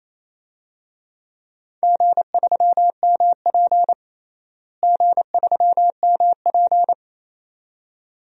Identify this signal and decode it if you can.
Morse code, heard as a keyed tone, about 28 wpm, 700 Hz. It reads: G3MP G3MP
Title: DXCC-100 Call Signs - 4 Characters - Repeated 2x - 28wpm